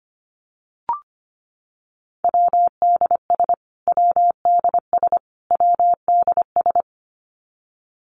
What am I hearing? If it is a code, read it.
Morse code, heard as a keyed tone, about 25 wpm, 700 Hz. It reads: WDS WDS WDS
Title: Abbreviations Repeated 3x - 25wpm